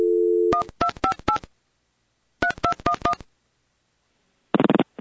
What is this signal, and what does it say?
DTMF, heard as telephone touch tones, tones of about 90 ms. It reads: 1258 3211
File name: 258-3211-ring